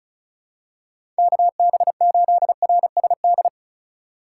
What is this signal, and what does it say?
Morse code, heard as a keyed tone, about 36 wpm, 700 Hz. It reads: KB8RSD
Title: USA Call Signs - 3x Character Spacing - 35wpm